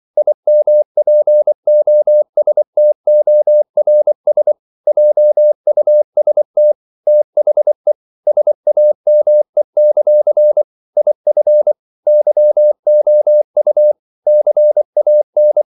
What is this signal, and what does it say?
Morse code, heard as a keyed tone, about 24 wpm, 600 Hz. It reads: IMPOSTORS JUST THE SAME; IF YOU CAN
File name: CW / Practice / If24